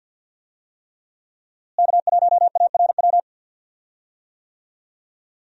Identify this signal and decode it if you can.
Morse code, heard as a keyed tone, about 47 wpm, 700 Hz. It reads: K1ARW
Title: USA Call Signs - 5 Characters - Repeated 3x - 50wpm